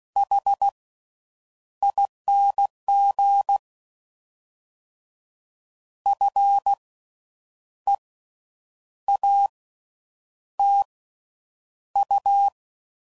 Morse code, heard as a keyed tone, about 16 wpm, spaced as for 8 wpm, 800 Hz. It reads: HING FEATU
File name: Slow With Faster Ings